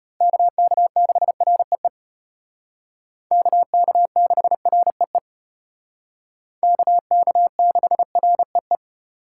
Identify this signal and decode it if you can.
Morse code, heard as a keyed tone, about 32 wpm, 700 Hz. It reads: KK6REE KK6REE KK6REE